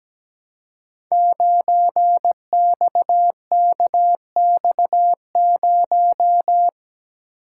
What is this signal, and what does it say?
Morse code, heard as a keyed tone, about 17 wpm, 700 Hz. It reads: 9XKX0
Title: Code Groups - 5 Letter:Numbers - 17wpm